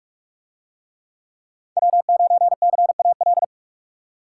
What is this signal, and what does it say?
Morse code, heard as a keyed tone, about 44 wpm, 700 Hz. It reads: W9CAL